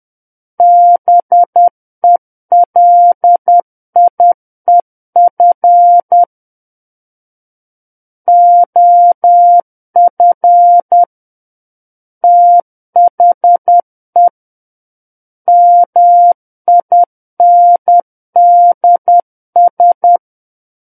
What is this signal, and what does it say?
Morse code, heard as a keyed tone, about 10 wpm, 700 Hz. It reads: BELIEF OF THE MINDS
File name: Book One / 01-Chapter 1 - 10 WPM